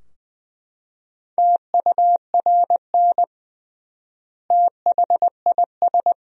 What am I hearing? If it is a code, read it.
Morse code, heard as a keyed tone, about 20 wpm, 700 Hz. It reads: TURN THIS
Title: 3x Word-Spacing - Sentences from Top 200 Words - 20wpm